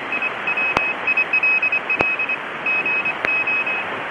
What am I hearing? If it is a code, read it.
Morse code, heard as a keyed tone, about 24 wpm, 2550 Hz, with noise. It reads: I WILL GO
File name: SWL / anteriores / 7060-wGOATs